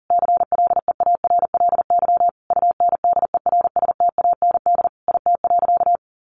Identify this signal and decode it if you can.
Morse code, heard as a keyed tone, about 40 wpm, 700 Hz. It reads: CLEARLY UNDERSTAND IT.
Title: Sentences from Top 1400 Words 40wpm